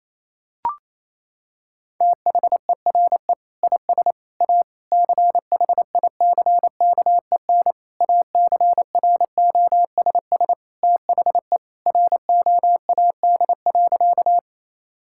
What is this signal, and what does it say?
Morse code, heard as a keyed tone, about 28 wpm, 700 Hz. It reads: THERE IS A CHICKEN ACROSS THE ROAD.
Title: Sentences from Top 1000 Words - Review Part 5 - 28wpm